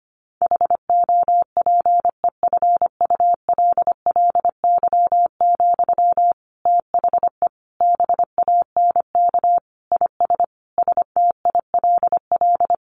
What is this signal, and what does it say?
Morse code, heard as a keyed tone, about 25 wpm, 700 Hz. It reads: HOPEFULLY, THE BANK IS STILL
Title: Sentences from Top 1500 Words 25wpm